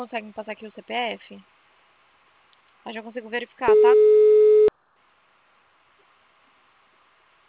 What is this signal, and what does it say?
Ringback tone (Europe, 425 Hz), with speech.